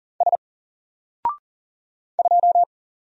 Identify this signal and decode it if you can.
Morse code, heard as a keyed tone, about 40 wpm, 700 Hz. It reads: S 2
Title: Single Character (Letter:Number:Punctuation:Prosign) Alt 40wpm